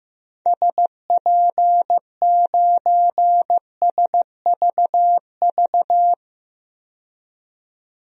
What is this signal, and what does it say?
Morse code, heard as a keyed tone, about 16 wpm, 700 Hz. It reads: SP9SVV
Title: DXCC-100 Call Signs - 6 Characters - 15wpm